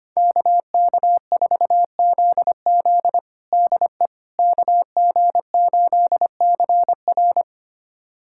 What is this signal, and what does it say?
Morse code, heard as a keyed tone, about 25 wpm, 700 Hz. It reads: KK4ZZ DE KG8CR
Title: USA Callsign DE Callsign - 5 Characters - 25wpm